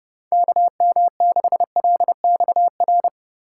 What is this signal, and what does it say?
Morse code, heard as a keyed tone, about 40 wpm, 700 Hz. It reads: KM6LXR